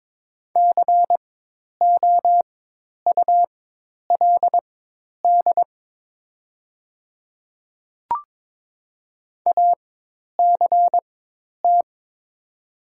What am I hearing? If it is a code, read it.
Morse code, heard as a keyed tone, about 23 wpm, spaced as for 10 wpm, 700 Hz. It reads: COULD ACT
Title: 4x Character Spacing - Top 300 Words - 22wpm